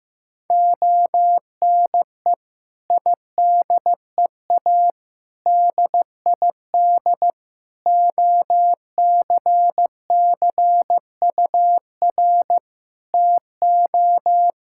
Morse code, heard as a keyed tone, about 16 wpm, 700 Hz. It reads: ONE IDEA DID OCCUR TO
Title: Sentences from Top 1000 Words - Review Part 6 - 15wpm